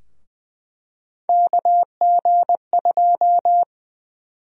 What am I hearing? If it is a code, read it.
Morse code, heard as a keyed tone, about 20 wpm, 700 Hz. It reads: KG2
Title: USA Call Sign Prefixes - 20wpm